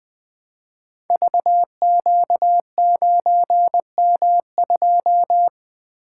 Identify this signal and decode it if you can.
Morse code, heard as a keyed tone, about 20 wpm, 700 Hz. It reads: VQ9M2
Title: Code Groups - 5 Letter:Numbers - 20wpm